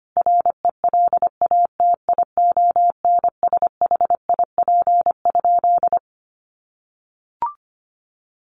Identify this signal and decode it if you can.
Morse code, heard as a keyed tone, about 25 wpm, 700 Hz. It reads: RELATIONSHIP?
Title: Sentences from Top 1100 Words 25wpm